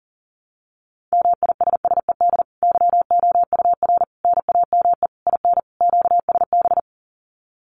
Morse code, heard as a keyed tone, about 40 wpm, 700 Hz. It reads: MISSED YOUR NAME IN QSB
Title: Common QSO Phrases 40wpm